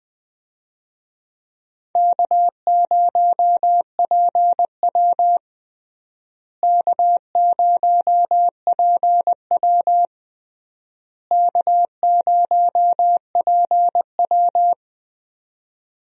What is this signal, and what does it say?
Morse code, heard as a keyed tone, about 20 wpm, 700 Hz. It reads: K0PW K0PW K0PW
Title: USA Call Signs - 4 Characters - Repeated 3x - 20wpm